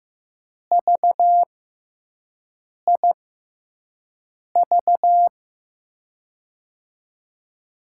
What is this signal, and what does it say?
Morse code, heard as a keyed tone, about 15 wpm, spaced as for 5 wpm, 700 Hz. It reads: VIV
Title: USA Call Signs - 6 Characters - 6x Character Spacing - 15wpm